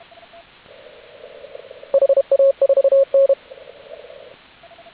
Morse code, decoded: HA4N

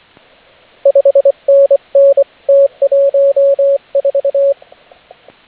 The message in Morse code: 5NN T14

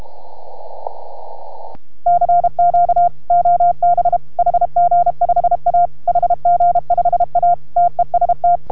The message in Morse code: CQ OB HG5A HG5A TEST

